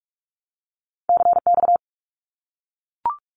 Morse code, heard as a keyed tone, about 44 wpm, 700 Hz. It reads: CX
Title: Abbreviations 45wpm